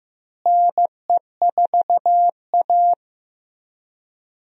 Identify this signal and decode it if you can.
Morse code, heard as a keyed tone, about 16 wpm, 700 Hz. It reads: NE4A
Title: USA Call Signs - 4 Characters - Repeated 3x - 15wpm